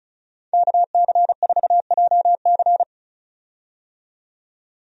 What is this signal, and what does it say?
Morse code, heard as a keyed tone, about 34 wpm, 700 Hz. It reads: KC4JC
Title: USA Call Signs - 5 Characters - 35wpm